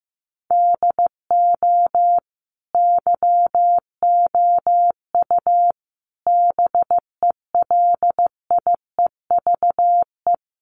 Morse code, heard as a keyed tone, about 15 wpm, 700 Hz. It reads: DO YOU BELIEVE
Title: Sentences from Top 1000 Words 15wpm